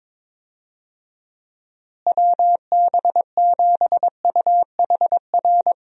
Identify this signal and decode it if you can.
Morse code, heard as a keyed tone, about 22 wpm, 700 Hz. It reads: WB7UHR
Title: USA Call Signs - 6 Characters - Repeated 3x - 22wpm